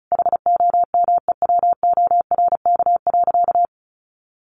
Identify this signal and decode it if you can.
Morse code, heard as a keyed tone, about 34 wpm, 700 Hz. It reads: HOMEWORK.
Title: Additional Easy Phrases 35wpm